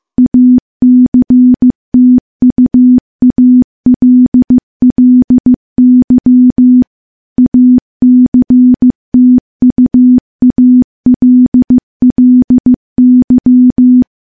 Morse code, heard as a keyed tone, about 15 wpm, 250 Hz. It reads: ACTUALLY ACTUALLY
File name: morse / morse-training-15WPM